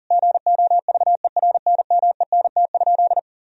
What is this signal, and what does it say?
Morse code, heard as a keyed tone, about 40 wpm, 700 Hz. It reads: GOVERNMENT?
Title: Sentences from Top 1000 Words - Review Part 4 - 40wpm